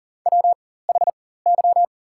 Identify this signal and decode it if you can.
Morse code, heard as a keyed tone, about 40 wpm, spaced as for 19 wpm, 700 Hz. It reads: WHY
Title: 4x Character Spacing - Top 200 Words - 40wpm